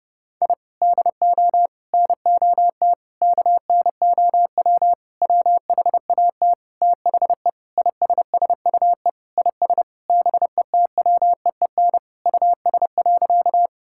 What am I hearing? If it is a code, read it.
Morse code, heard as a keyed tone, about 30 wpm, 700 Hz. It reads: I DO NOT KNOW WHAT THE ISSUE IS BETWEEN US.
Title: Sentences from Top 1000 Words - Review Part 6 - 30wpm